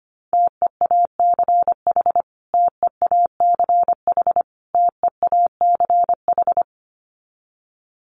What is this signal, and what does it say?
Morse code, heard as a keyed tone, about 25 wpm, 700 Hz. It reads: TEACH TEACH TEACH
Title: Top 500 Words Repeated 25wpm